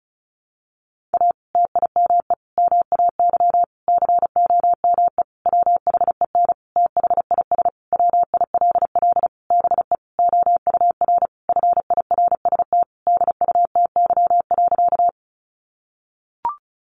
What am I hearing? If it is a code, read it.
Morse code, heard as a keyed tone, about 34 wpm, 700 Hz. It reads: A TIME MAY COME WHEN THIS WILL BE OUR FIRST DUTY.
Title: Sentences from Top 1500 Words 35wpm